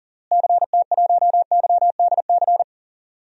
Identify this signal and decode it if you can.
Morse code, heard as a keyed tone, about 37 wpm, 700 Hz. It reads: CT1YDC
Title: DXCC-100 Call Signs - 6 Characters - Repeated 3x - 40wpm